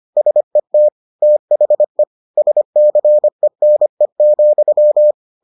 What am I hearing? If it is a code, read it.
Morse code, heard as a keyed tone, about 25 wpm, 600 Hz. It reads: SET THE SCENE,